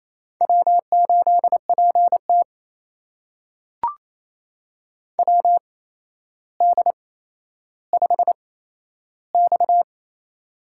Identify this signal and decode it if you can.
Morse code, heard as a keyed tone, about 28 wpm, spaced as for 12 wpm, 700 Hz. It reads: W8PT WD5X
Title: USA Call Signs - 4 Characters - 8x Character Spacing - 28wpm